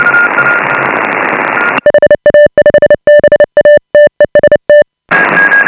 Morse code, heard as a keyed tone, about 29 wpm, 590 Hz. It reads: HA5BA TEST